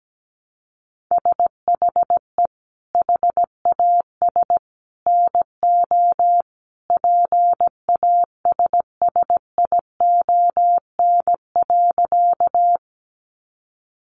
Morse code, heard as a keyed tone, about 17 wpm, 700 Hz. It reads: SHE HAS NO PASSION.
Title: Sentences from Top 1300 Words 17wpm